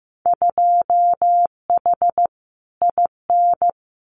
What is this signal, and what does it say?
Morse code, heard as a keyed tone, about 15 wpm, 700 Hz. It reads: 2H IN
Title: Winter Field Day Exchanges Encore 15wpm